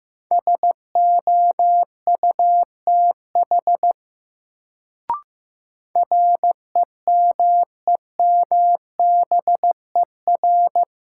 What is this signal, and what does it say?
Morse code, heard as a keyed tone, about 15 wpm, 700 Hz. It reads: SOUTH REMEMBER
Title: Sentences from Top 400 Words - New Word Drill - 15wpm